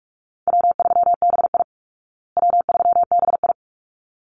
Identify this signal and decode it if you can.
Morse code, heard as a keyed tone, about 44 wpm, 700 Hz. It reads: W3BI W3BI
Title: Easy Call Signs Repeated 2x - 45wpm